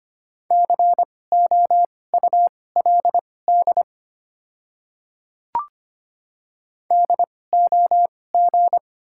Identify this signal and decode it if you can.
Morse code, heard as a keyed tone, about 25 wpm, spaced as for 18 wpm, 700 Hz. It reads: COULD DOG
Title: 2x Character Spacing - Top 500 Words - 25wpm